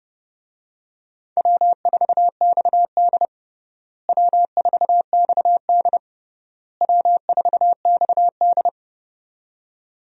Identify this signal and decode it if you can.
Morse code, heard as a keyed tone, about 29 wpm, 700 Hz. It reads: W4XD W4XD W4XD
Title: USA Call Signs - 4 Characters - Repeated 3x - 30wpm